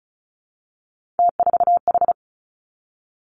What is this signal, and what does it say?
Morse code, heard as a keyed tone, about 36 wpm, 700 Hz. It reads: T4H
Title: Three Challenging Letter-Number Combinations 35wpm